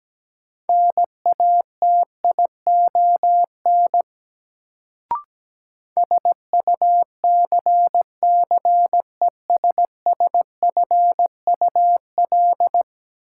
Morse code, heard as a keyed tone, about 17 wpm, 700 Hz. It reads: NATION SUCCESSFUL
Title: Sentences from Top 900 Words - New Word Drill - 17wpm